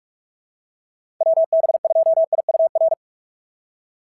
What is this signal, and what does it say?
Morse code, heard as a keyed tone, about 44 wpm, 680 Hz. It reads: WB2IUR
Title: SOTA QSO Practice 45wpm